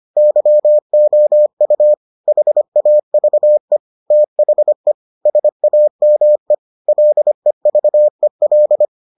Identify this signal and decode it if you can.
Morse code, heard as a keyed tone, about 25 wpm, 600 Hz. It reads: YOU HAVE THE SAME LEVEL